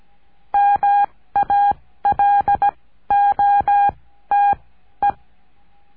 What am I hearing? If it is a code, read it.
Morse code, heard as a keyed tone, about 17 wpm, spaced as for 13 wpm, 810 Hz. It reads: MALOTE
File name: cursos / CURSO DE TELEGRAFIA / Telegrafia - Trilha 14